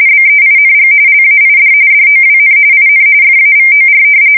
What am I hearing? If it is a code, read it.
RTTY, about 45.45 baud. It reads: BEING LAID DOWN IN FAVOR
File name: fsk170 / battleships